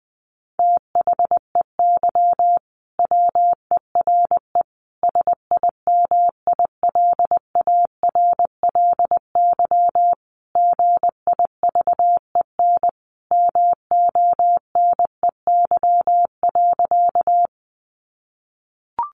Morse code, heard as a keyed tone, about 19 wpm, 700 Hz. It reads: THEY WERE SIMILARLY GIVEN MONEY.
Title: Sentences from Top 1400 Words 20wpm